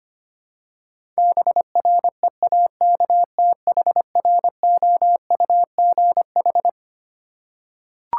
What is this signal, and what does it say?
Morse code, heard as a keyed tone, about 25 wpm, 700 Hz. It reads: BREAKTHROUGH
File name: Twelve Letter Words - 25wpm